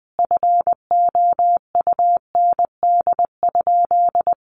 Morse code, heard as a keyed tone, about 20 wpm, 700 Hz. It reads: FOUND?
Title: Sentences from Top 300 Words 20wpm